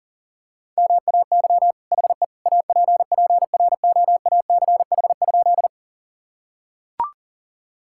Morse code, heard as a keyed tone, about 40 wpm, 700 Hz. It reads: MAY HE APPROACH?
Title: Sentences from Top 1000 Words - Review Part 6 - 40wpm